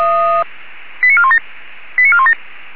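Five-tone selective call (ZVEI), heard as quick runs of five tones, tones of about 70 ms. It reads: 89417 89417
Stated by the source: ZVEI five-tone call received via SDR.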